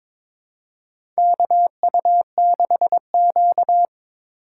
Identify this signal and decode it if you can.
Morse code, heard as a keyed tone, about 22 wpm, 700 Hz. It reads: KU6Q
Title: USA Call Signs - 4 Characters - 4x Character Spacing - 22wpm